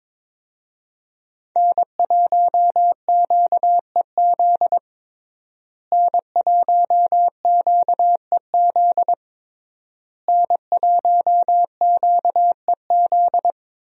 Morse code, heard as a keyed tone, about 22 wpm, 700 Hz. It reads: N1QEZ N1QEZ N1QEZ
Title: USA Call Signs - 5 Characters - Repeated 3x - 22wpm